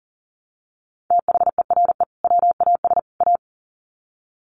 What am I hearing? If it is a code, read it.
Morse code, heard as a keyed tone, about 40 wpm, 700 Hz. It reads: THERE WAS A
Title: Sets of 3 Words 40wpm